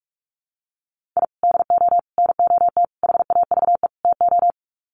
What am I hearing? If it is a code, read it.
Morse code, heard as a keyed tone, about 44 wpm, 700 Hz. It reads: I DO NOT HAVE TO
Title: Sets of 5 Words 45wpm